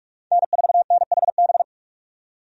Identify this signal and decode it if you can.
Morse code, heard as a keyed tone, about 47 wpm, 700 Hz. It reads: N4NHB
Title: USA Call Signs - 5 Characters - 45wpm